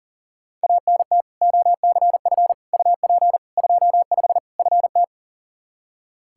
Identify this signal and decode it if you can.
Morse code, heard as a keyed tone, about 40 wpm, 700 Hz. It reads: ANT OCF UP 25 FT
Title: Practice QSO - Exchange 2 - Info - 40wpm